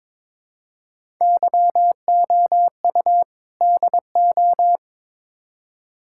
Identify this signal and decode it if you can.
Morse code, heard as a keyed tone, about 22 wpm, 700 Hz. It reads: YOU DO
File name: practice / 22wpm / Build A Phrase - 5-Word Phrases - 22wpm